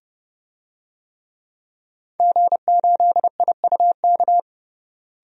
Morse code, heard as a keyed tone, about 30 wpm, 700 Hz. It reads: G8IUK
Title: DXCC-100 Call Signs - 5 Characters - Repeated 3x - 30wpm